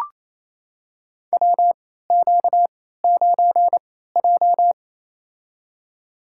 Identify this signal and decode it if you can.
Morse code, heard as a keyed tone, about 28 wpm, spaced as for 16 wpm, 700 Hz. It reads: WQ9J
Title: USA Call Signs - 4 Characters - 3x Character Spacing - 28wpm